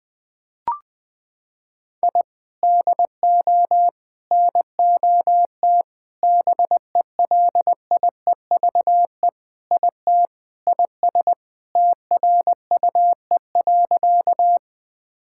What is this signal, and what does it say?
Morse code, heard as a keyed tone, about 21 wpm, 700 Hz. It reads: I DO NOT BELIEVE IT IS TRUE.
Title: Sentences with 5-Word Phrases - 20wpm